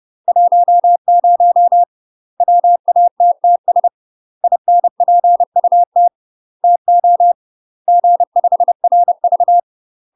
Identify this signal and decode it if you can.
Morse code, heard as a keyed tone, about 30 wpm, 700 Hz. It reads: 10 WATTS INPUT TO G5RV
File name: UK MASTER QSOs TRAINING / 30WPM / UK MASTER QSOs 0046